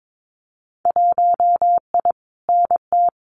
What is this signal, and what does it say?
Morse code, heard as a keyed tone, about 22 wpm, 700 Hz. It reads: 1I NT
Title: Winter Field Day Exchanges Encore 22wpm